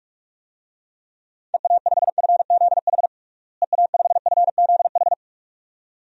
Morse code, heard as a keyed tone, about 43 wpm, 700 Hz. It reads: EA5FZH EA5FZH
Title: DXCC-100 Call Signs - 6 Characters - Repeated 2x - 45wpm